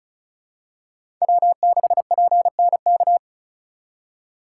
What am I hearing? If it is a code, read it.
Morse code, heard as a keyed tone, about 34 wpm, 700 Hz. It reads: W6PNK
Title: USA Call Signs - 5 Characters - 6x Character Spacing - 35wpm